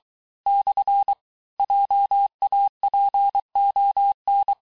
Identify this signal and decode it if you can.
Morse code, heard as a keyed tone, about 23 wpm, 800 Hz. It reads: / JAPON
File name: texte-clair-vitesse-17mpm-puis-22mpm